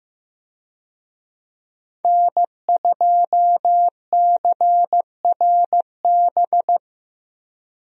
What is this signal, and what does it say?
Morse code, heard as a keyed tone, about 15 wpm, 700 Hz. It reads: N2CRB
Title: USA Call Signs - 5 Characters - Repeated 2x - 15wpm